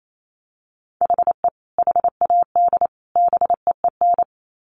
Morse code, heard as a keyed tone, about 28 wpm, 700 Hz. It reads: HE HAD BEEN
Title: Sets of 3 Words 28wpm